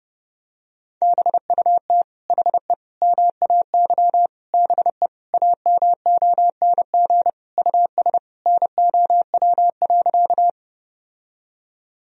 Morse code, heard as a keyed tone, about 30 wpm, 700 Hz. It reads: BUT HE MAY BE AMONG US NOW.
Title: Sentences from Top 1000 Words - Review Part 3 - 30wpm